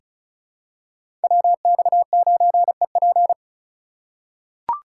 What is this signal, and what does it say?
Morse code, heard as a keyed tone, about 36 wpm, 700 Hz. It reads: WX9EP
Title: USA Call Signs - 5 Characters - 4x Character Spacing - 35wpm